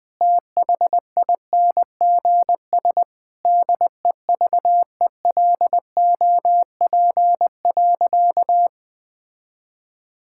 Morse code, heard as a keyed tone, about 20 wpm, 700 Hz. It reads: THINGS DEVELOP.
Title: Sentences with 6-Word Phrases - 20wpm